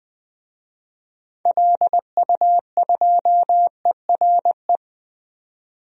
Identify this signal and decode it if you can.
Morse code, heard as a keyed tone, about 20 wpm, 700 Hz. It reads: LU2ERE
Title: DXCC-100 Call Signs - 6 Characters - 20wpm